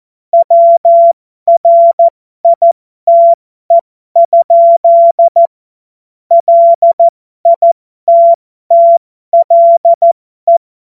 Morse code, heard as a keyed tone, about 14 wpm, spaced as for 12 wpm, 680 Hz. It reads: WRITE? LITTLE
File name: Morsekurs / Witze